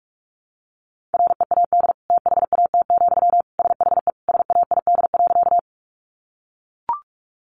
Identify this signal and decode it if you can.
Morse code, heard as a keyed tone, about 44 wpm, 700 Hz. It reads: READ THAT, SHE SAID.
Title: Sentences from Top 200 Words 45wpm